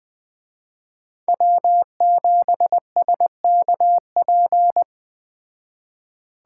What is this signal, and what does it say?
Morse code, heard as a keyed tone, about 20 wpm, 700 Hz. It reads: W7SKP